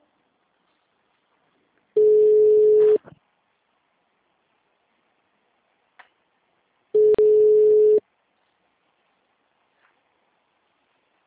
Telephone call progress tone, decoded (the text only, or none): ringback tone